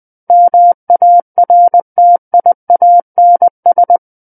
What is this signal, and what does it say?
Morse code, heard as a keyed tone, about 20 wpm, 700 Hz. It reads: MARTIANS
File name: 20 WPM